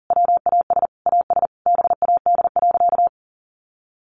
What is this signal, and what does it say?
Morse code, heard as a keyed tone, about 40 wpm, 700 Hz. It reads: WAS AS BAD.
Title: Sentences from Top 1200 Words 40wpm